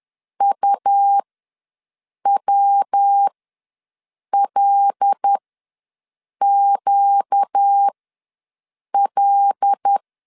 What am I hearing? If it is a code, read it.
Morse code, heard as a keyed tone, about 10 wpm, spaced as for 6 wpm, 800 Hz. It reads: UWLQL
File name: LEZIONI / 5 LBVC / 5 L ws 10wpm 3636